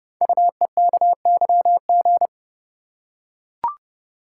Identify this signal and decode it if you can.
Morse code, heard as a keyed tone, about 28 wpm, 700 Hz. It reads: UEKYG
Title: Code Groups - 5 Letters - 30wpm